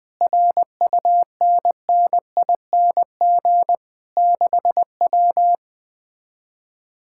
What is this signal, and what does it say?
Morse code, heard as a keyed tone, about 20 wpm, 700 Hz. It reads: RUNNING 6W
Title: Speed-Racing Common QSO Phrases Encore - 17wpm